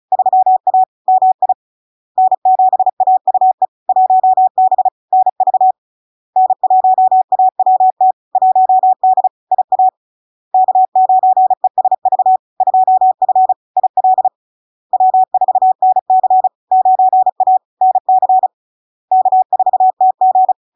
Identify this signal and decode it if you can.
Morse code, heard as a keyed tone, about 34 wpm, 750 Hz. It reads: 3A MI N7AUE 1B NV N1AWT 1D IA K9ESV 2F IL W4NC 9A NC K4TG
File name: LARK-FD-2022-Log-in-CW-35wpm-750-Hz